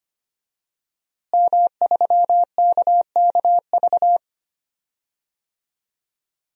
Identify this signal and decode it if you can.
Morse code, heard as a keyed tone, about 25 wpm, 700 Hz. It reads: M3KKV
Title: DXCC-100 Call Signs - 5 Characters - Repeated 3x - 25wpm